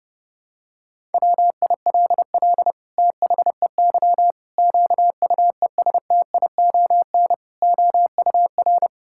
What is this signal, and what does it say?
Morse code, heard as a keyed tone, about 32 wpm, 700 Hz. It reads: WILL THEY QUESTION OUR
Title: Sentences from Top 1000 Words - Review Part 5 - 30wpm